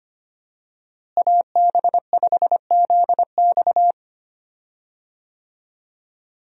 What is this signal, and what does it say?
Morse code, heard as a keyed tone, about 25 wpm, 700 Hz. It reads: AB5ZX
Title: USA Call Signs - 5 Characters - Repeated 3x - 25wpm